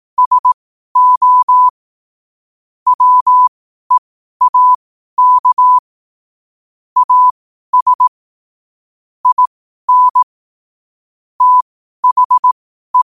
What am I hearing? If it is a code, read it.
Morse code, heard as a keyed tone, about 18 wpm, spaced as for 12 wpm, 1000 Hz. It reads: SO WEAK AS IN THE
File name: qotdfiles / 12wpm